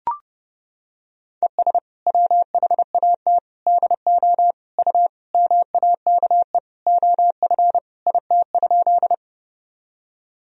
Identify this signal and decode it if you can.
Morse code, heard as a keyed tone, about 32 wpm, 700 Hz. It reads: ES WHAT DO U MAKE OF IT?
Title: Sentences from Top 100 Words and Abbreviations 30wpm